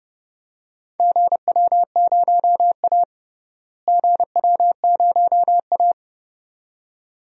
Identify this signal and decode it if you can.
Morse code, heard as a keyed tone, about 30 wpm, 700 Hz. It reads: GW0A GW0A
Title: DXCC-100 Call Signs - 4 Characters - Repeated 2x - 30wpm